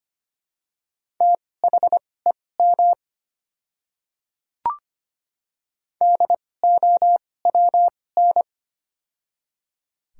Morse code, heard as a keyed tone, about 25 wpm, spaced as for 18 wpm, 700 Hz. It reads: THEM DOWN
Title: 2x Character Spacing - Top 100 Words - 25wpm